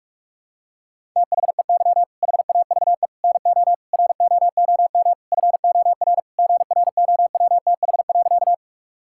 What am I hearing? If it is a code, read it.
Morse code, heard as a keyed tone, about 44 wpm, 700 Hz. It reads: THEY HAVE NO ROOM FOR GROWTH.
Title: Sentences from Top 1000 Words - Review Part 5 - 45wpm